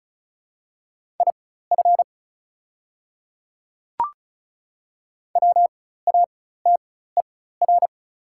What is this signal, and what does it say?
Morse code, heard as a keyed tone, about 34 wpm, spaced as for 16 wpm, 700 Hz. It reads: IF WATER
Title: 4x Character Spacing - Top 100 Words - 35wpm